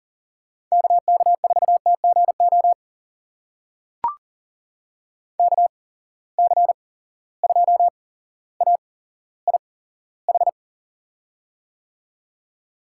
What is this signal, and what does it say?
Morse code, heard as a keyed tone, about 40 wpm, spaced as for 17 wpm, 700 Hz. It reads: KK4TGO KC2AIH
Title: USA Call Signs - 6 Characters - 8x Character Spacing - 40wpm